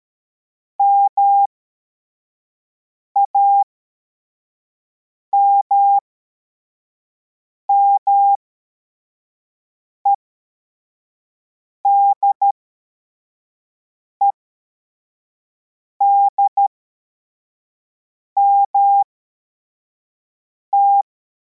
Morse code, heard as a keyed tone, about 13 wpm, spaced as for 4 wpm, 800 Hz. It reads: MAMME DEDMT
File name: formation / cw / Text23